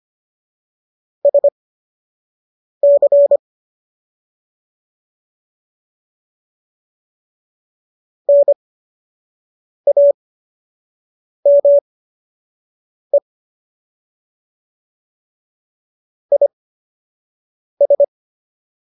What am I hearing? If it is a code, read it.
Morse code, heard as a keyed tone, about 25 wpm, spaced as for 6 wpm, 570 Hz. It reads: SC NAME IS